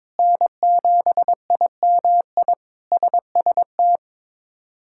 Morse code, heard as a keyed tone, about 22 wpm, 700 Hz. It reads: N7IMI SST
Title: K1USN SST QSO Practice - 22wpm